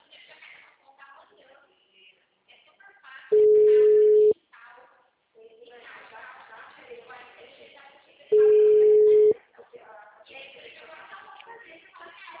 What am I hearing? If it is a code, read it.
Ringback tone (Europe, 425 Hz), with speech.
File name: out- -2011- - - .244